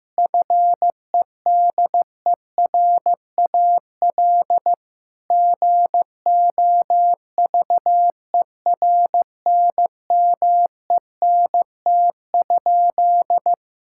Morse code, heard as a keyed tone, about 15 wpm, 700 Hz. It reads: FEDERAL GOVERNMENT?
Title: Sentences from Top 1000 Words - Review Part 5 - 15wpm